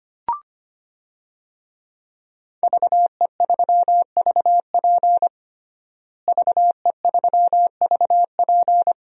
Morse code, heard as a keyed tone, about 25 wpm, 700 Hz. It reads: VE3VP VE3VP
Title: DXCC-100 Call Signs - 5 Characters - Repeated 2x - 25wpm